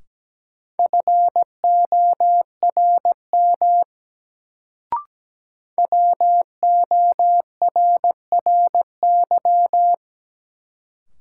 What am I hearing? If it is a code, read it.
Morse code, heard as a keyed tone, about 17 wpm, 700 Hz. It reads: FORM WORRY
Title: Sentences from Top 1100 Words - New Word Drill Alt - 17wpm